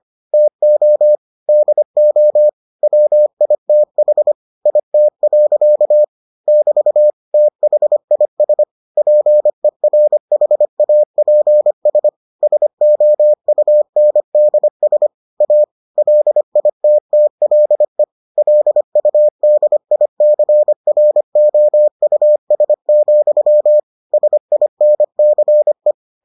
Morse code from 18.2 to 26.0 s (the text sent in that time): LUDICROUS, SINCE